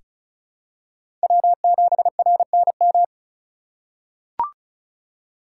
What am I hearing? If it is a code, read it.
Morse code, heard as a keyed tone, about 34 wpm, 700 Hz. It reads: W7RNM
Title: USA Call Signs - 3x Character Spacing - 35wpm